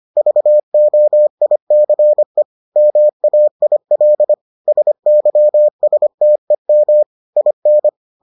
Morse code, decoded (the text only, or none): VOICE MAIL SYSTEM IN